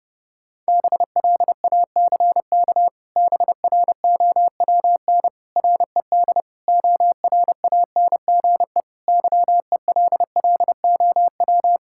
Morse code, heard as a keyed tone, about 30 wpm, 700 Hz. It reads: BLACK BROWN RED ORANGE YELLOW
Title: Pangram Challenge v2 - 30wpm